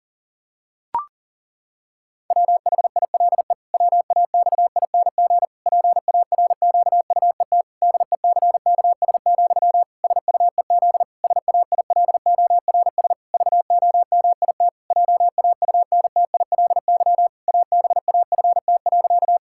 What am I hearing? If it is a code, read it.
Morse code, heard as a keyed tone, about 37 wpm, 700 Hz. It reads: WHILE WAXING PARQUET DECKS, SUEZ SAILORS VOMIT JAUNTILY ABAFT.
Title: Pangram Challenge v2 - 40wpm